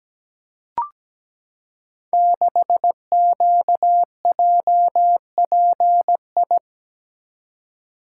Morse code, heard as a keyed tone, about 17 wpm, 700 Hz. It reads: 6QJPI
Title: Code Groups - 5 Letter:Numbers - 17wpm